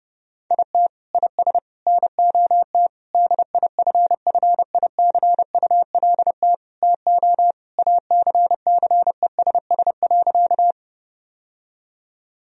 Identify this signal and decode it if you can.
Morse code, heard as a keyed tone, about 32 wpm, 700 Hz. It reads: IT IS NOT DIFFICULT TO ACCESS.
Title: Sentences from Top 1000 Words - Review Part 4 - 30wpm